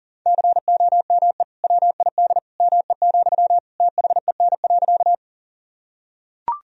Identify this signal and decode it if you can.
Morse code, heard as a keyed tone, about 37 wpm, 700 Hz. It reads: COME WID ME, THEN.
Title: Sentences from Top 200 Words and Abbreviations 40wpm